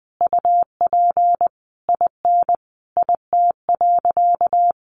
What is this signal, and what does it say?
Morse code, heard as a keyed tone, about 20 wpm, 700 Hz. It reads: UP IN IT.
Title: Sentences from Top 1500 Words 20wpm